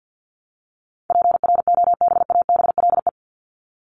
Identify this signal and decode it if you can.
Morse code, heard as a keyed tone, about 50 wpm, 700 Hz. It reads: PROBABLE
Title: LY Suffix - Top 100 Words with Base Words 50wpm